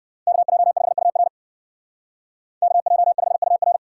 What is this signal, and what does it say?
Morse code, heard as a keyed tone, about 56 wpm, 700 Hz. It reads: KJ5UR KJ5UR
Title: Speed Racing - USA 2x2 Call Signs 45wpm